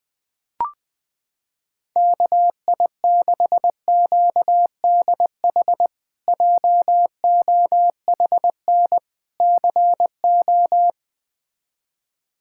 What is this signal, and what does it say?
Morse code, heard as a keyed tone, about 20 wpm, 700 Hz. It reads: KI6QDH JOHN CO
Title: K1USN SST QSO Practice - 20wpm